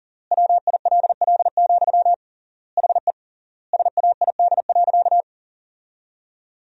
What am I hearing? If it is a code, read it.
Morse code, heard as a keyed tone, about 40 wpm, 700 Hz. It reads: WILL, HE SAID.
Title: 3x Word-Spacing - Sentences from Top 200 Words - 40wpm